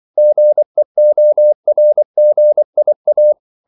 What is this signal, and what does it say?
Morse code, heard as a keyed tone, about 24 wpm, 600 Hz. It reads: GEORGIA
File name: CW / Practice / NationalCapitals24